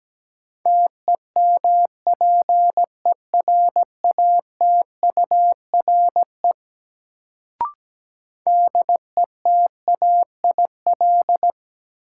Morse code, heard as a keyed tone, about 17 wpm, 700 Hz. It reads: TEMPERATURE DETAIL
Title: Sentences from Top 1500 Words - New Word Drill Alt - 17wpm